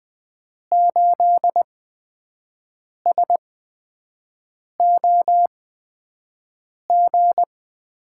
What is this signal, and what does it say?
Morse code, heard as a keyed tone, about 20 wpm, spaced as for 5 wpm, 700 Hz. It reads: 8SOG
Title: USA Call Signs - 6 Characters - 8x Character Spacing - 20wpm